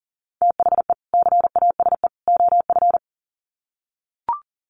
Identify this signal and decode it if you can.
Morse code, heard as a keyed tone, about 40 wpm, 700 Hz. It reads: THE CASE OF
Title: Sets of 3 Words 40wpm